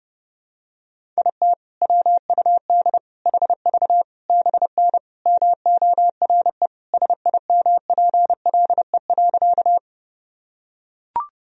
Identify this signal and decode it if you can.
Morse code, heard as a keyed tone, about 30 wpm, 700 Hz. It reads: IT WUD HV BN MORE SIMPLE.
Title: Sentences from Top 400 Words and Abbreviations 30wpm